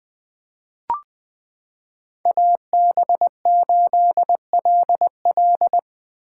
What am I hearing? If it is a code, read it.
Morse code, heard as a keyed tone, about 20 wpm, 700 Hz. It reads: AB8LL
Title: 2023 POTA Hunter Call Signs - 20wpm